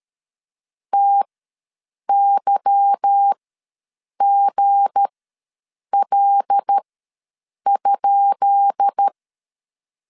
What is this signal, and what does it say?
Morse code, heard as a keyed tone, about 13 wpm, spaced as for 7 wpm, 800 Hz. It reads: TYGL?
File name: LEZIONI / 10 SIGNS / 10B INT ws 12wpm 3527